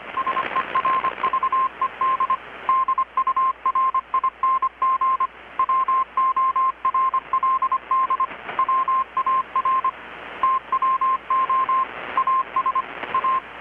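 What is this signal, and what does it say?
Morse code, heard as a keyed tone, about 25 wpm, 1050 Hz, with radio noise. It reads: SERVED DURING WORLD WAR TWO AS A
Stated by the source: Morse from KPH's Night of Nights XX maritime radiotelegraphy event, received in Chiapas, Mexico about 3,800 km away on a Kenwood R-600 with a grounded 10 m long-wire antenna.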